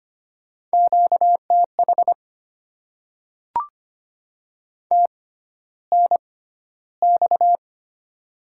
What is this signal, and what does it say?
Morse code, heard as a keyed tone, about 25 wpm, spaced as for 13 wpm, 700 Hz. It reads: QTH TNX
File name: Top 20 QSO Elements - 6x Character Spacing - 25wpm